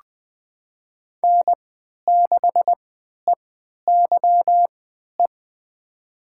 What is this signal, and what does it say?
Morse code, heard as a keyed tone, about 20 wpm, spaced as for 11 wpm, 700 Hz. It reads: N6EYE
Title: USA Call Signs - 5 Characters - 3x Character Spacing - 20wpm